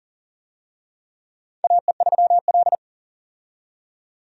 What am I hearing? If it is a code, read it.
Morse code, heard as a keyed tone, about 42 wpm, 690 Hz. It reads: AE3L
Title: USA Call Signs - 4 Characters - 40wpm